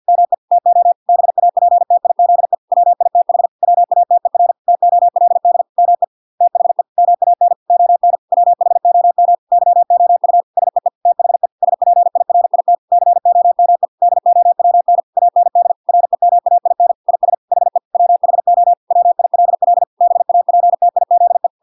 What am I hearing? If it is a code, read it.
Morse code, heard as a keyed tone, about 50 wpm, 700 Hz. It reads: ME TO BAPTIZE WITH WATER TOLD ME THE MAN ON WHOM YOU SEE THE SPIRIT COME DOWN AND REMAIN IS HE WHO WILL BAPTIZE